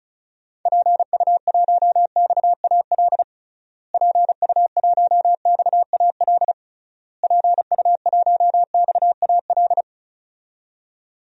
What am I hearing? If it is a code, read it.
Morse code, heard as a keyed tone, about 36 wpm, 700 Hz. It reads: PU1XAL PU1XAL PU1XAL
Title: DXCC-100 Call Signs - 6 Characters - Repeated 3x - 35wpm